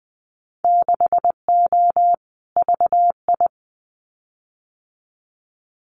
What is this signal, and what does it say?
Morse code, heard as a keyed tone, about 20 wpm, 700 Hz. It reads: 6O VI
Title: Winter Field Day Exchanges Encore 20wpm